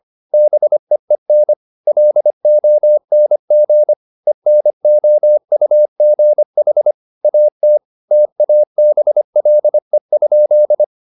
Morse code, heard as a keyed tone, about 25 wpm, 600 Hz. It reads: BEEN LONG ENOUGH AT TABLE?
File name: the old man and the sea part 1 25wpm